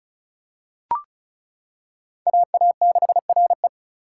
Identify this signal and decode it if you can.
Morse code, heard as a keyed tone, about 36 wpm, 700 Hz. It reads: AA6RE
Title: K1USN SST QSO Practice - 35wpm